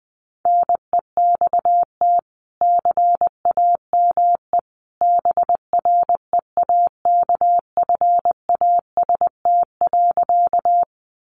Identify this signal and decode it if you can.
Morse code, heard as a keyed tone, about 20 wpm, 700 Hz. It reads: NEXT CAME BREAKFAST.
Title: Sentences from Top 1500 Words 20wpm